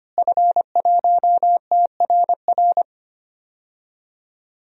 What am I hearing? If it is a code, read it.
Morse code, heard as a keyed tone, about 25 wpm, 700 Hz. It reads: F1TRR
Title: DXCC-100 Call Signs - 5 Characters - Repeated 3x - 25wpm